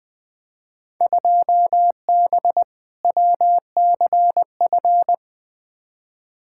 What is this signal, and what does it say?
Morse code, heard as a keyed tone, about 20 wpm, 700 Hz. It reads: 2B WCF
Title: Field Day Exchanges Encore - 20wpm